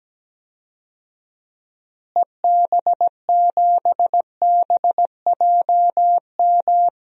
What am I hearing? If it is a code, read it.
Morse code, heard as a keyed tone, about 17 wpm, 700 Hz. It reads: EB7BJM
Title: DXCC-100 Call Signs - 6 Characters - Repeated 3x - 17wpm